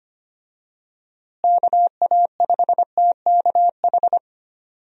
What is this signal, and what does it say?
Morse code, heard as a keyed tone, about 25 wpm, 700 Hz. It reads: KA5TKH
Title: USA Call Signs - 6 Characters - Repeated 2x - 25wpm